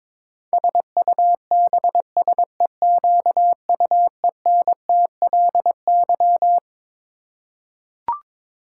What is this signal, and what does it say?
Morse code, heard as a keyed tone, about 22 wpm, 700 Hz. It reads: SUBSEQUENTLY
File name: practice / 22wpm / Twelve Letter Words - 22wpm